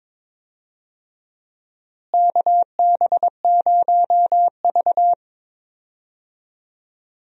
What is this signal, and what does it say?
Morse code, heard as a keyed tone, about 22 wpm, 700 Hz. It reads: KB0V
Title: USA Call Signs - 4 Characters - Repeated 3x - 22wpm